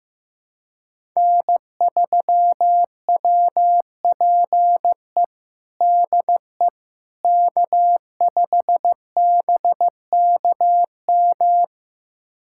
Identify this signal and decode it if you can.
Morse code, heard as a keyed tone, about 16 wpm, 700 Hz. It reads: N3WPE DE K5BKM
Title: USA Callsign DE Callsign - 5 Characters - 15wpm